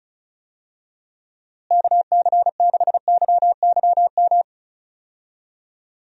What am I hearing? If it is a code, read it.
Morse code, heard as a keyed tone, about 34 wpm, 700 Hz. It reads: KC6YYM
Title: USA Call Signs - 6 Characters - Repeated 2x - 35wpm